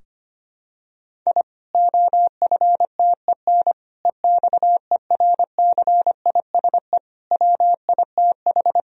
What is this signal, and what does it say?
Morse code, heard as a keyed tone, about 25 wpm, 700 Hz. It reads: I OFTEN EXERCISE WITH
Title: Sentences from Top 1000 Words - Review Part 4 - 25wpm